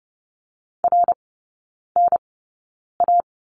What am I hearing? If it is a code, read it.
Morse code, heard as a keyed tone, about 30 wpm, spaced as for 9 wpm, 700 Hz. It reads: RNA
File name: 081 Lesson 14e - Number 2 - Three Characters 30wpm